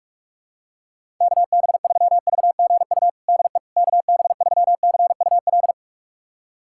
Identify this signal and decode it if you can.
Morse code, heard as a keyed tone, about 43 wpm, 700 Hz. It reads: KB3VGU DE KB3CUL
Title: USA Callsign DE Callsign 45wpm